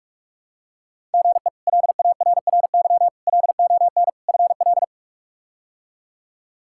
Morse code, heard as a keyed tone, about 41 wpm, 700 Hz. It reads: GE LARRY LON FL
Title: K1USN SST QSO Practice - 45wpm